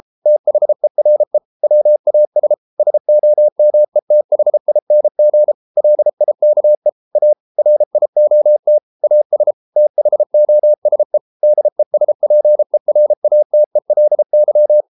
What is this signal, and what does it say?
Morse code, heard as a keyed tone, about 33 wpm, 600 Hz. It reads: THERE WAS SOMETHING LIKE A RIOT AS THOSE DESPERATELY